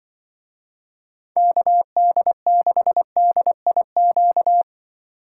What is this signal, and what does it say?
Morse code, heard as a keyed tone, about 24 wpm, 700 Hz. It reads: KD6DIQ
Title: Speed Racing - USA Call Signs 20wpm